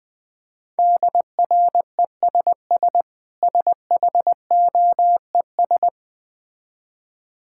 Morse code, heard as a keyed tone, about 20 wpm, 700 Hz. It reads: DRESS SHOES
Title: Difficult Words with Hints - 20wpm